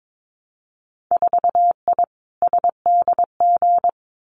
Morse code, heard as a keyed tone, about 22 wpm, 700 Hz. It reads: 4I SDG
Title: Winter Field Day Exchanges Encore 22wpm